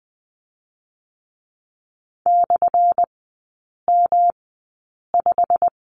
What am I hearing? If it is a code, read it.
Morse code, heard as a keyed tone, about 20 wpm, spaced as for 8 wpm, 690 Hz. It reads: /M5
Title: 198 Lesson 32e - Letter k - Three Characters 20wpm